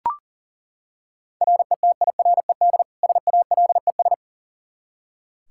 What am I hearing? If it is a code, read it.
Morse code, heard as a keyed tone, about 37 wpm, 690 Hz. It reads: RETIRED SALES
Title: Practice QSO - Exchange 3 - More Info - 40wpm